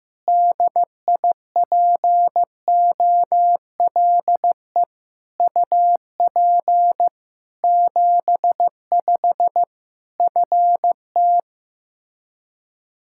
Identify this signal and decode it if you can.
Morse code, heard as a keyed tone, about 16 wpm, 700 Hz. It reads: DIPOLE UP 75 FT
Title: K1USN SST QSO Practice 15wpm